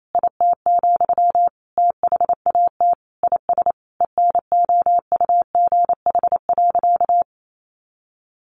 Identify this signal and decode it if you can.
Morse code, heard as a keyed tone, about 28 wpm, 700 Hz. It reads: IT, THAT IS ENOUGH.
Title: Sentences from Top 400 Words 28wpm